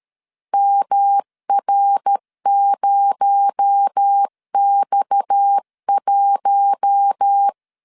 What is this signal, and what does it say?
Morse code, heard as a keyed tone, about 13 wpm, 800 Hz. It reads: MR0X1
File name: CW / DET / 1 6 1500C S B 12wpm 3431